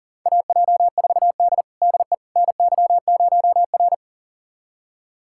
Morse code, heard as a keyed tone, about 40 wpm, 700 Hz. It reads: AJ4D DE NY0R
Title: USA Callsign DE Callsign - 4 Characters - 40wpm